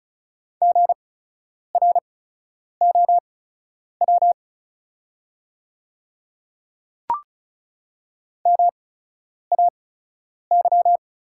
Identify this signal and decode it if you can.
Morse code, heard as a keyed tone, about 34 wpm, spaced as for 10 wpm, 700 Hz. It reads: GROW MAY
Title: 8x Character Spacing - Top 300 Words - 35wpm